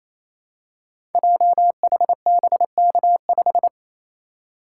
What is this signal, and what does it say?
Morse code, heard as a keyed tone, about 29 wpm, 700 Hz. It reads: JHBK5
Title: Code Groups - 5 Letter:Numbers - 28wpm